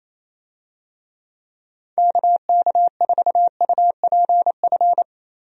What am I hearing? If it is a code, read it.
Morse code, heard as a keyed tone, about 27 wpm, 700 Hz. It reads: KK4UPF
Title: USA Call Signs - 6 Characters - 28wpm